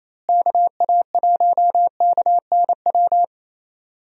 Morse code, heard as a keyed tone, about 28 wpm, 700 Hz. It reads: KA1KNW
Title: USA Call Signs - 6 Characters - Repeated 2x - 28wpm